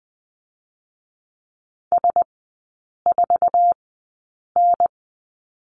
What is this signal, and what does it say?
Morse code, heard as a keyed tone, about 20 wpm, spaced as for 8 wpm, 690 Hz. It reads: S4N
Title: 060 Lesson 11e - Letter h - Three Characters 20wpm